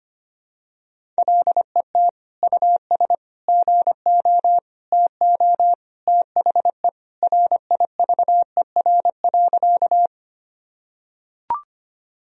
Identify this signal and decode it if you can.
Morse code, heard as a keyed tone, about 25 wpm, 700 Hz. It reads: LET US GO TO THE RIVER.
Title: Sentences from Top 1000 Words - Review Part 1 - 25wpm